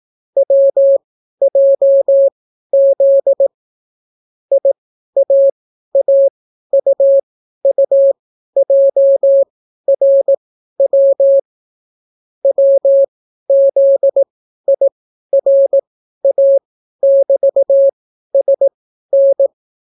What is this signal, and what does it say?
Morse code, heard as a keyed tone, about 18 wpm, spaced as for 12 wpm, 550 Hz. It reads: WJZ IAAUUJRW WZIRA<BT>SN